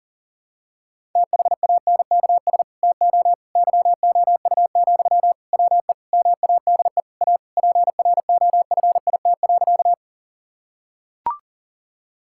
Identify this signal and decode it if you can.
Morse code, heard as a keyed tone, about 40 wpm, 700 Hz. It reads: THANKS TO YOU, WE MADE A PROFIT.
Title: Sentences from Top 1000 Words - Review Part 5 - 40wpm